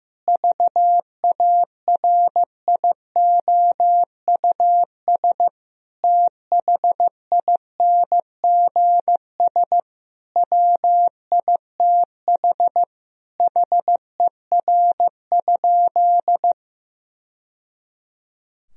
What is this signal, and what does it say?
Morse code, heard as a keyed tone, about 16 wpm, 700 Hz. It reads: VARIOUS THINGS WITH HER?
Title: Sentences from Top 1000 Words - Review Part 4 - 15wpm